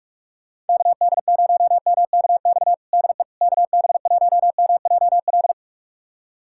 Morse code, heard as a keyed tone, about 46 wpm, 700 Hz. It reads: KD0MKX DE KB1MJL
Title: USA Callsign DE Callsign - 6 Characters - 45wpm